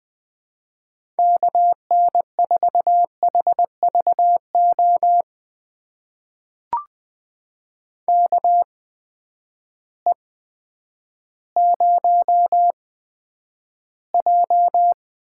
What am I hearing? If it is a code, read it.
Morse code, heard as a keyed tone, about 20 wpm, spaced as for 10 wpm, 700 Hz. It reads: KN4HVO KE0J
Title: USA Call Signs - 6 Characters - 8x Character Spacing - 20wpm